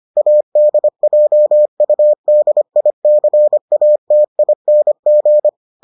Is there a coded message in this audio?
Yes: Morse code from 0.2 to 5.5 s, heard as a keyed tone, about 25 wpm, 600 Hz. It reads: ADJUDICATING